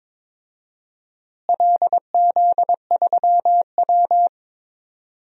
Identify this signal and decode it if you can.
Morse code, heard as a keyed tone, about 22 wpm, 700 Hz. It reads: LZ3W
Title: DXCC-100 Call Signs - 4 Characters - Repeated 3x - 22wpm